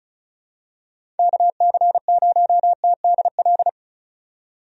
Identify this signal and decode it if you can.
Morse code, heard as a keyed tone, about 36 wpm, 700 Hz. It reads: KC0TDL
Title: USA Call Signs - 6 Characters - 35wpm